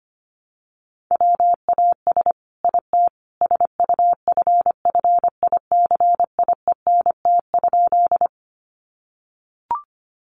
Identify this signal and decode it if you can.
Morse code, heard as a keyed tone, about 25 wpm, 700 Hz. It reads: WAS IT SUFFICIENT?
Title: Sentences from Top 1400 Words 25wpm